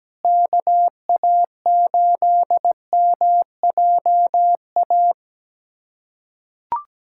Morse code, heard as a keyed tone, about 17 wpm, 700 Hz. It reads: KA8MJA
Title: USA Callsign DE Callsign 17wpm